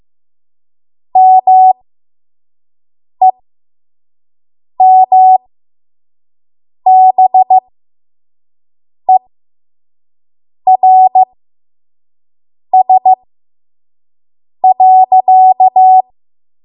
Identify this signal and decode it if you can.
Morse code, heard as a keyed tone, about 15 wpm, spaced as for 5 wpm, 750 Hz. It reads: MEMBERS.
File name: Morse / 05WPM